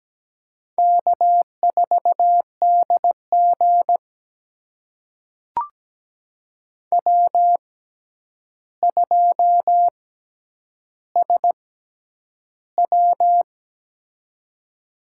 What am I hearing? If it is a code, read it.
Morse code, heard as a keyed tone, about 17 wpm, spaced as for 9 wpm, 700 Hz. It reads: K4DG W2SW
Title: USA Call Signs - 6x Character Spacing - 17wpm